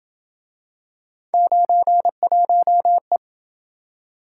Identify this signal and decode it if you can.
Morse code, heard as a keyed tone, about 27 wpm, 700 Hz. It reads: 91E